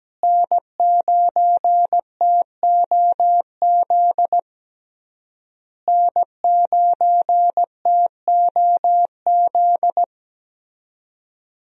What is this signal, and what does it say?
Morse code, heard as a keyed tone, about 17 wpm, 700 Hz. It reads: N9TOZ N9TOZ
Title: USA Call Signs - 5 Characters - Repeated 3x - 17wpm